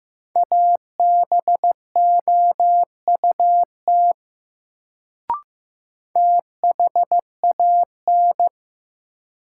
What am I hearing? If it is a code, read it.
Morse code, heard as a keyed tone, about 15 wpm, 700 Hz. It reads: ABOUT THAN
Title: Sentences from Top 100 Words - New Word Drill - 15wpm